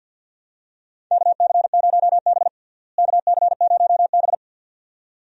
Morse code, heard as a keyed tone, about 53 wpm, 700 Hz. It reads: KC0B KC0B
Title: USA Call Signs - 4 Characters - Repeated 2x - 50wpm